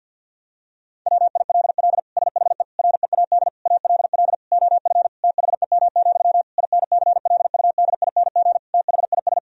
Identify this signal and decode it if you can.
Morse code, heard as a keyed tone, about 50 wpm, 700 Hz. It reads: WILL SHE READ ALL OF THEM, INCLUDING THIS
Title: Sentences from Top 1000 Words - Review Part 4 - 50wpm